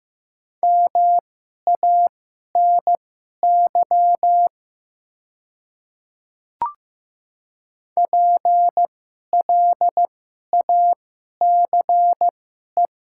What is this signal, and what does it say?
Morse code, heard as a keyed tone, about 15 wpm, spaced as for 11 wpm, 700 Hz. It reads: MANY PLACE
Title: 2x Character Spacing - Top 200 Words - 15wpm